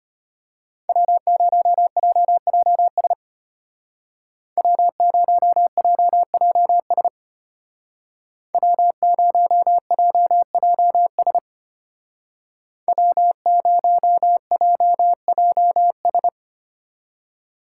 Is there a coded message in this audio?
Yes: Morse code from 0.9 to 16.3 s, heard as a keyed tone, about 33 wpm, 700 Hz. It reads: W0JJS W0JJS W0JJS W0JJS